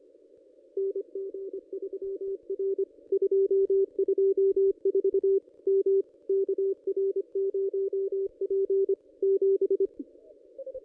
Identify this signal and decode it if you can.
Morse code, heard as a keyed tone, about 25 wpm, 390 Hz. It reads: NG3R 224 M KR0P 7
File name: ss-cw-k5tr-op-n5rz